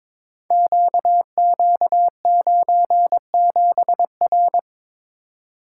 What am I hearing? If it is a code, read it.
Morse code, heard as a keyed tone, about 22 wpm, 700 Hz. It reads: QQ97R
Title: Code Groups - 5 Letter:Numbers - 22wpm